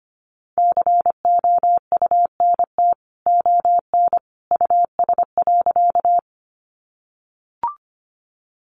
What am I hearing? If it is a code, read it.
Morse code, heard as a keyed tone, about 25 wpm, 700 Hz. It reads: COUNT ON US.
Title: Sentences from Top 1500 Words 25wpm